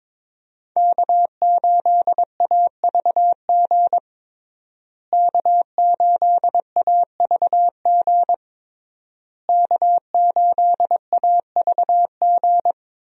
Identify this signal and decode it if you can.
Morse code, heard as a keyed tone, about 22 wpm, 700 Hz. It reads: K8AVG K8AVG K8AVG
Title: USA Call Signs - 5 Characters - Repeated 3x - 22wpm